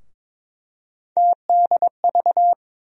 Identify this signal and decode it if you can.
Morse code, heard as a keyed tone, about 22 wpm, 700 Hz. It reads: TDV
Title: USA Call Sign Postfixes - 3 Letters 22wpm